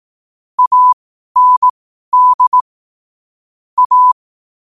Morse code, heard as a keyed tone, about 18 wpm, spaced as for 12 wpm, 1000 Hz. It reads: AND A
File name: qotdfiles / 12wpm